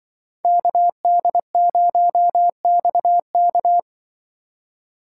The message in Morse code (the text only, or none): KD0XK